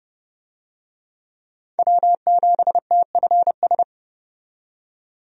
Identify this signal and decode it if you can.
Morse code, heard as a keyed tone, about 32 wpm, 700 Hz. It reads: W7TFS
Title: USA Call Signs - 5 Characters - 30wpm